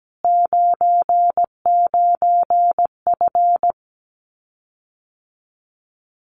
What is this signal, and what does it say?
Morse code, heard as a keyed tone, about 17 wpm, 700 Hz. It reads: 99F
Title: Common QSO Phrases Encore 17wpm